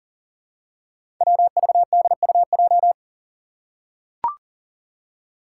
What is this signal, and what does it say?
Morse code, heard as a keyed tone, about 37 wpm, 700 Hz. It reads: WVDUJ
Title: Five Challenging Letter-Number Combinations 40wpm